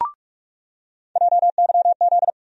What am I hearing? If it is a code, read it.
Morse code, heard as a keyed tone, about 46 wpm, 700 Hz. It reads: JYZ
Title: USA Call Sign Postfixes - 3 Letters 45wpm